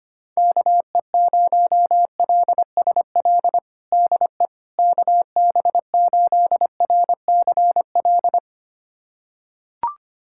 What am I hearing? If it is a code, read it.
Morse code, heard as a keyed tone, about 25 wpm, 700 Hz. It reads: KE0LSL DE KB8RCL
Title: USA Callsign DE Callsign 25wpm